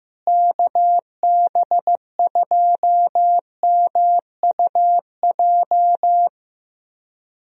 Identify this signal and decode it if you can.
Morse code, heard as a keyed tone, about 15 wpm, 700 Hz. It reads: KB2MUJ
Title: USA Call Signs - 6 Characters - Repeated 3x - 15wpm